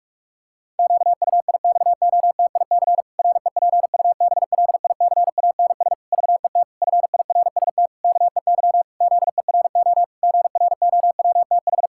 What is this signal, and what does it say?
Morse code, heard as a keyed tone, about 46 wpm, 700 Hz. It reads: QUIXOTIC REPUBLICANS VET FIRST KEY ZERO GROWTH
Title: Pangram Challenge v2 - 45wpm